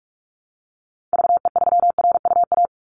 Morse code, heard as a keyed tone, about 46 wpm, 700 Hz. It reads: VE3RUA
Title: Call Signs - 45wpm